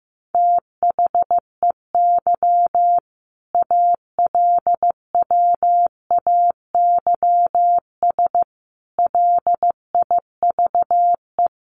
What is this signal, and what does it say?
Morse code, heard as a keyed tone, about 14 wpm, 700 Hz. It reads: THEY ALWAYS LIVE
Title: Sentences from Top 1400 Words 15wpm